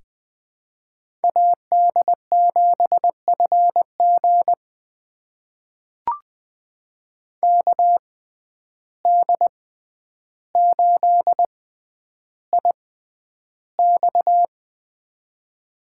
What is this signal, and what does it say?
Morse code, heard as a keyed tone, about 20 wpm, spaced as for 10 wpm, 700 Hz. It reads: AD7FG KD8IX
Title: USA Call Signs - 5 Characters - 6x Character Spacing - 20wpm